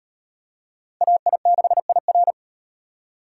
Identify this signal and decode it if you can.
Morse code, heard as a keyed tone, about 35 wpm, 700 Hz. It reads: AI6IR